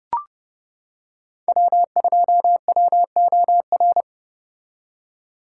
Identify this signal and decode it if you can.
Morse code, heard as a keyed tone, about 29 wpm, 700 Hz. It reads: W2WOR